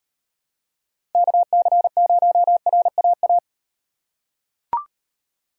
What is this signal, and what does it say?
Morse code, heard as a keyed tone, about 38 wpm, 700 Hz. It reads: KC0RAA